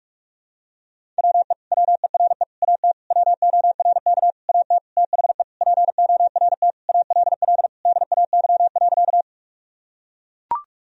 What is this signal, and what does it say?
Morse code, heard as a keyed tone, about 42 wpm, 700 Hz. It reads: WE WERE AT WORK AT THE PORT ALL DAY.
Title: Sentences from Top 1000 Words - Review Part 1 - 45wpm